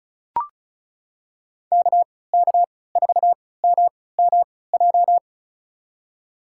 Morse code, heard as a keyed tone, about 36 wpm, spaced as for 20 wpm, 700 Hz. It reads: KK4MMJ
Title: USA Call Signs - 3x Character Spacing - 35wpm